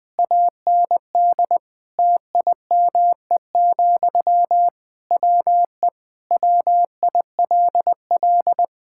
Morse code, heard as a keyed tone, about 20 wpm, 700 Hz. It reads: AND TIME, WE WILL
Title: Sentences from Top 1000 Words - Review Part 5 - 20wpm